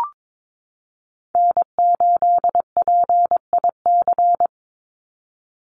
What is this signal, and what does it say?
Morse code, heard as a keyed tone, about 22 wpm, 700 Hz. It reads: N8PIC
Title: USA Call Signs 22wpm